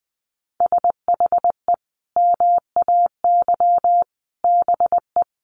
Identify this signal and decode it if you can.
Morse code, heard as a keyed tone, about 20 wpm, 700 Hz. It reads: SHE MAY BE
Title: Sentences from Top 1400 Words 20wpm